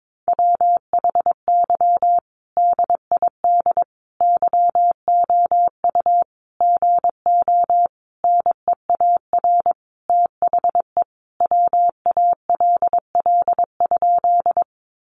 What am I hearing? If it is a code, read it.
Morse code, heard as a keyed tone, about 22 wpm, 700 Hz. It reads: WHY DID YOU GO NEAR THE WALL?
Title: Sentences from Top 1100 Words 22wpm